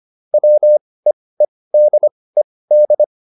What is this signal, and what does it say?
Morse code, heard as a keyed tone, about 25 wpm, spaced as for 18 wpm, 600 Hz. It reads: WEEDED